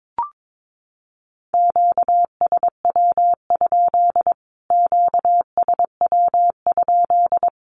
Morse code, heard as a keyed tone, about 22 wpm, 700 Hz. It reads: QSW? QSW?
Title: Q-Codes Repeated 3x 22wpm